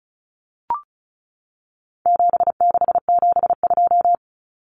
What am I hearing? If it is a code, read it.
Morse code, heard as a keyed tone, about 34 wpm, 700 Hz. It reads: 7672
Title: Common Cut Numbers - 4-Digit 35wpm